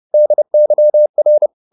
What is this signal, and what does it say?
Morse code, heard as a keyed tone, about 30 wpm, 600 Hz. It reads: DYR
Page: Ferie 30 wpm